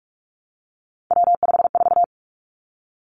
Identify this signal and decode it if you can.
Morse code, heard as a keyed tone, about 46 wpm, 700 Hz. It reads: W54
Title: Three Challenging Letter-Number Combinations 45wpm